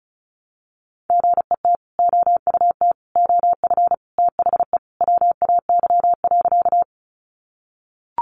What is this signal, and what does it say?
Morse code, heard as a keyed tone, about 34 wpm, 700 Hz. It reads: GET OUT OF THE WAY.
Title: Sentences from Top 200 Words 35wpm